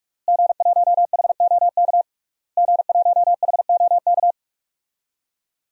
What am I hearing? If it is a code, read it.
Morse code, heard as a keyed tone, about 41 wpm, 700 Hz. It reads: G1HOK G1HOK
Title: DXCC-100 Call Signs - 5 Characters - Repeated 3x - 45wpm